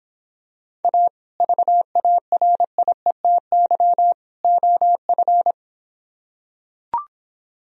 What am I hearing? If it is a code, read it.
Morse code, heard as a keyed tone, about 26 wpm, 700 Hz. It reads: A VARIETY OF